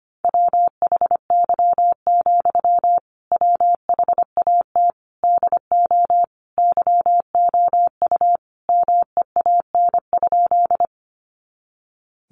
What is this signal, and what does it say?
Morse code, heard as a keyed tone, about 25 wpm, 700 Hz. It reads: WHY, WHAT DO YOU MEAN?
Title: Sentences from Top 200 Words 25wpm